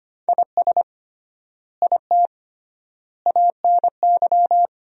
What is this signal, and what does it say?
Morse code, heard as a keyed tone, about 25 wpm, 700 Hz. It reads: IS IT ANY
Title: 3x Word-Spacing - Sentences from Top 200 Words - 25wpm